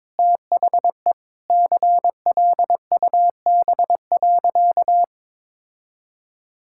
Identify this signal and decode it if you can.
Morse code, heard as a keyed tone, about 22 wpm, 700 Hz. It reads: THE CLUB.
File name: practice / 22wpm / Sentences with 3-Word Phrases - 22wpm